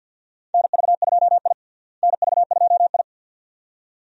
Morse code, heard as a keyed tone, about 51 wpm, 700 Hz. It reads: NV2I NV2I
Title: USA Call Signs - 4 Characters - Repeated 2x - 50wpm